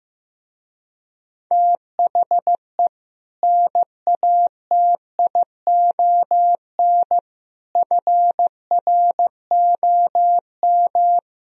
Morse code, heard as a keyed tone, about 15 wpm, 700 Hz. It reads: THE NATION FROM
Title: Speed-Racing - Sets of 5 Words 15wpm